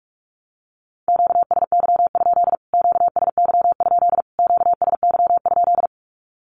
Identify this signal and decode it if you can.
Morse code, heard as a keyed tone, about 46 wpm, 700 Hz. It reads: QSY? QSY? QSY?
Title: Q-Codes Repeated 3x 45wpm